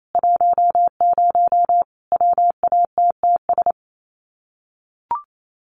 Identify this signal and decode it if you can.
Morse code, heard as a keyed tone, about 28 wpm, 700 Hz. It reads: 10 WATTS
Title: Common QSO Phrases 28wpm